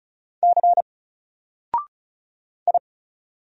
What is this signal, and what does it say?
Morse code, heard as a keyed tone, about 34 wpm, 700 Hz. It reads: C I
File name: Single-Letter-Number-35wpm